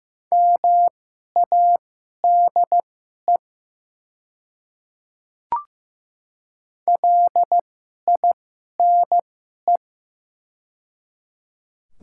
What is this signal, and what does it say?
Morse code, heard as a keyed tone, about 15 wpm, spaced as for 11 wpm, 700 Hz. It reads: MADE LINE
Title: 2x Character Spacing - Top 200 Words - 15wpm